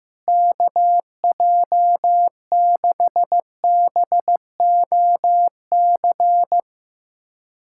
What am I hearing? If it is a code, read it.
Morse code, heard as a keyed tone, about 15 wpm, 700 Hz. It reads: KJ6BOC
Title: USA Callsign DE Callsign - 6 Characters - 15wpm